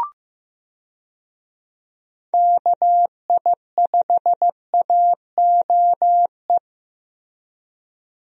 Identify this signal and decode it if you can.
Morse code, heard as a keyed tone, about 15 wpm, 700 Hz. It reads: KI5AOE
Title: USA Call Signs - 6 Characters - 15wpm